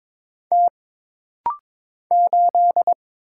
Morse code, heard as a keyed tone, about 22 wpm, 700 Hz. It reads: T 8
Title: Single Letter-Number - Rapid-Fire 22wpm